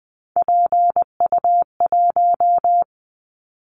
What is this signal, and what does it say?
Morse code, heard as a keyed tone, about 20 wpm, 700 Hz. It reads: PU1
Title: Callsign Prefixes for Top 100 DXCC - 20wpm